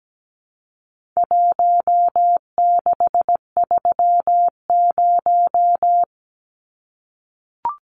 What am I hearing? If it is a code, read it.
Morse code, heard as a keyed tone, about 17 wpm, 700 Hz. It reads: 1630
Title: Numbers - Contest Style - 4-Digit 17wpm